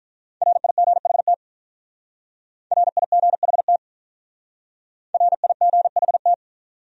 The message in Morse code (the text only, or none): RIGHT RIGHT RIGHT